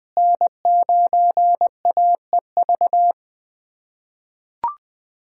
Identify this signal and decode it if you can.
Morse code, heard as a keyed tone, about 20 wpm, 700 Hz. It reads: N9AEV
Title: USA Callsign DE Callsign - 5 Characters - 20wpm